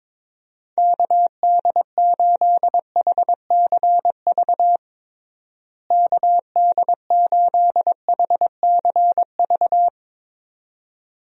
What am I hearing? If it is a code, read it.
Morse code, heard as a keyed tone, about 22 wpm, 700 Hz. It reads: KD8HCV KD8HCV
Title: USA Call Signs - 6 Characters - Repeated 2x - 22wpm